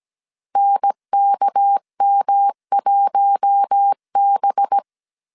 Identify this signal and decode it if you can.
Morse code, heard as a keyed tone, about 17 wpm, 800 Hz. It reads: NKM1B
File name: 16wpm